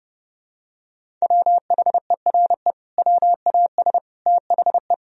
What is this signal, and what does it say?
Morse code, heard as a keyed tone, about 32 wpm, 700 Hz. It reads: WHERE WAS THE
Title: Sentences from Top 1000 Words - Review Part 1 - 30wpm